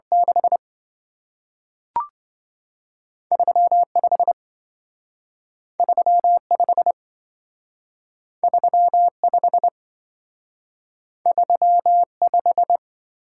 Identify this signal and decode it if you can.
Morse code, heard as a keyed tone, about 27 wpm, 700 Hz. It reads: 6 35 35 35 35